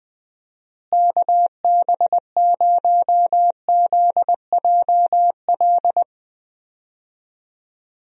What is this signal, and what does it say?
Morse code, heard as a keyed tone, about 20 wpm, 700 Hz. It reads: KB0ZJL
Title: USA Call Signs - 6 Characters - Repeated 2x - 20wpm